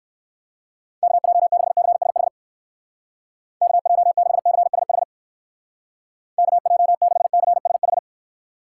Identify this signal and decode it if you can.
Morse code, heard as a keyed tone, about 57 wpm, 700 Hz. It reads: KJ6CSH KJ6CSH KJ6CSH